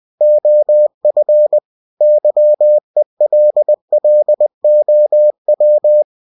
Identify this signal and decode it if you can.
Morse code, heard as a keyed tone, about 20 wpm, 600 Hz. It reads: OF YELLOW
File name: cw-books / winnie-the-pooh / winnie-20wpm-page-0072